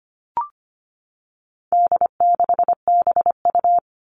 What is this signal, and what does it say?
Morse code, heard as a keyed tone, about 25 wpm, 700 Hz. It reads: D6BU
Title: All Cut Numbers - 4-Digit 25wpm